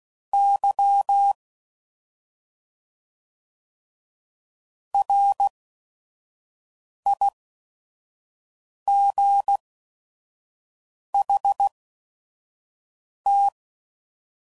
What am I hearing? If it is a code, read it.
Morse code, heard as a keyed tone, about 16 wpm, spaced as for 5 wpm, 800 Hz. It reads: Y RIGHT